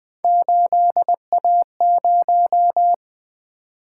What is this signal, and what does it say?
Morse code, heard as a keyed tone, about 20 wpm, 700 Hz. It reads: 8A0
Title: Three Letter-Number Combinations - 20wpm